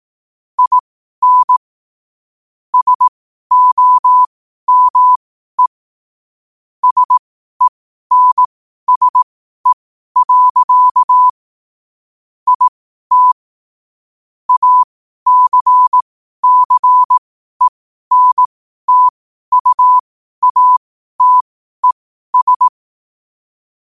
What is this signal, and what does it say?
Morse code, heard as a keyed tone, about 18 wpm, spaced as for 12 wpm, 1000 Hz. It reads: IN SOME SENSE. IT ACCENTUATES
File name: qotdfiles / 12wpm